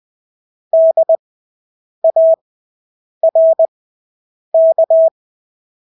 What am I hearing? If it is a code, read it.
Morse code, heard as a keyed tone, about 20 wpm, spaced as for 8 wpm, 650 Hz. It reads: DARK
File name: Engelske ord 20/8 wpm